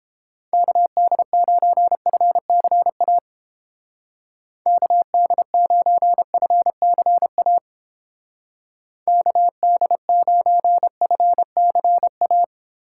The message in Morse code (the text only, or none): KD9FCA KD9FCA KD9FCA